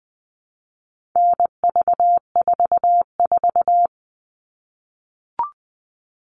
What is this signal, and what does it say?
Morse code, heard as a keyed tone, about 20 wpm, 700 Hz. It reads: NV44
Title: All Cut Numbers - 4-Digit 20wpm